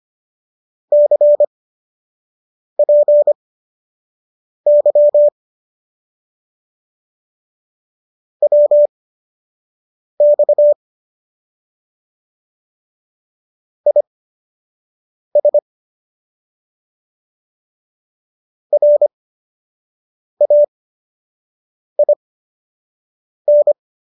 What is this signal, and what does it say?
Morse code, heard as a keyed tone, about 25 wpm, spaced as for 6 wpm, 590 Hz. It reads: CPY WX IS RAIN
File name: qso003 06